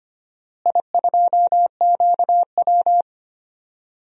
Morse code, heard as a keyed tone, about 25 wpm, 700 Hz. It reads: I2QW
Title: DXCC-100 Call Signs - 4 Characters - Repeated 3x - 25wpm